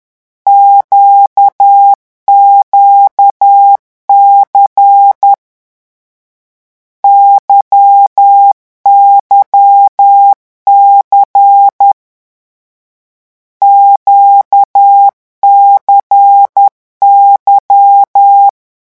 Morse code, heard as a keyed tone, about 10 wpm, 800 Hz. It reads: QQC YYC QCY
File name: morse mp3 / lecon2-vitesse 10